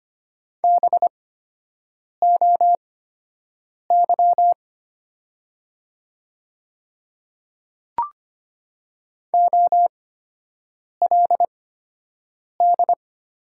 Morse code, heard as a keyed tone, about 25 wpm, spaced as for 7 wpm, 700 Hz. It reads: BOY OLD
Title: 8x Character Spacing - Top 200 Words - 25wpm